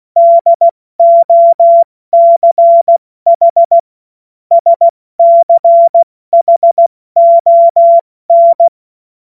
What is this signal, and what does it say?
Morse code, heard as a keyed tone, about 16 wpm, 680 Hz. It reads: DOCH SCHON
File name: Morsekurs / Witze / Witz 21. Dezember 2021, 70 F80